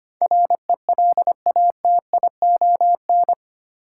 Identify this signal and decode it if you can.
Morse code, heard as a keyed tone, about 25 wpm, 700 Hz. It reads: RELATION
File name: 8 Letter Words - First 3 Letters Given - 25wpm